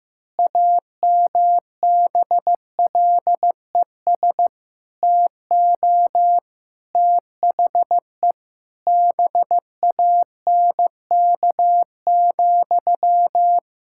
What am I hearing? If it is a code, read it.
Morse code, heard as a keyed tone, about 15 wpm, 700 Hz. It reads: AMBLES TO THE BANK,
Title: Pangram Challenge v2 - 15wpm